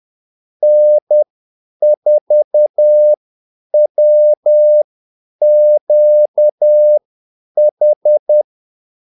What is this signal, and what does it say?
Morse code, heard as a keyed tone, about 10 wpm, spaced as for 8 wpm, 600 Hz. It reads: N4WQH
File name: Callsigns-10-8wpm